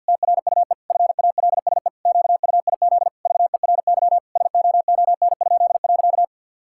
Morse code, heard as a keyed tone, about 50 wpm, 700 Hz. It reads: TRUE FALSE QUIZ VERY SOON?.
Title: Pangram Challenge v2 - 50wpm